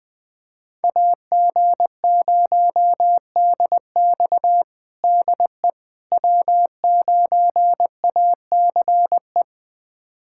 Morse code, heard as a keyed tone, about 20 wpm, 700 Hz. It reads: AG0DX DE W9ACE
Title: USA Callsign DE Callsign - 5 Characters - 20wpm